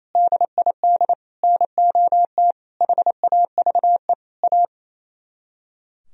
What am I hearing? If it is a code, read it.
Morse code, heard as a keyed tone, about 38 wpm, spaced as for 33 wpm, 700 Hz. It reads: DID NOT HAVE A
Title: Speed-Racing - Sets of 5 Words 28wpm